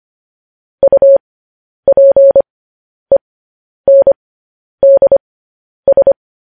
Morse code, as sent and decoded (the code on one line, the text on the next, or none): ..- .--. -. -.. ...
UPENDS